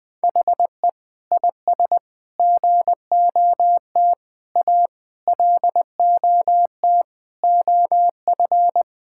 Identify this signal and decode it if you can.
Morse code, heard as a keyed tone, about 21 wpm, 700 Hz. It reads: HE IS GOT A LOT OF
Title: Sentences with 4-Word Phrases - 20wpm